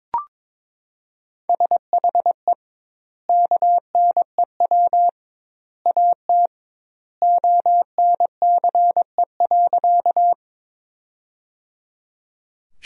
Morse code, heard as a keyed tone, about 22 wpm, 700 Hz. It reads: SHE KNEW AT ONCE.
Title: 2x Word-Spacing - Sentences from Top 400 Words - 22wpm